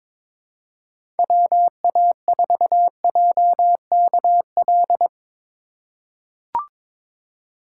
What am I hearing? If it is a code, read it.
Morse code, heard as a keyed tone, about 22 wpm, 700 Hz. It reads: WA4JKL
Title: K1USN SST QSO Practice - 22wpm